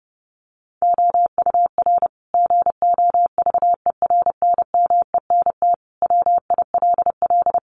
Morse code, heard as a keyed tone, about 30 wpm, 700 Hz. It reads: OUR GOVERNMENT WILL
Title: Sentences from Top 1200 Words 30wpm